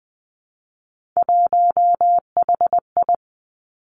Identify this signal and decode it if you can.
Morse code, heard as a keyed tone, about 19 wpm, 700 Hz. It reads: 1HI
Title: Three Challenging Letter-Number Combinations 20wpm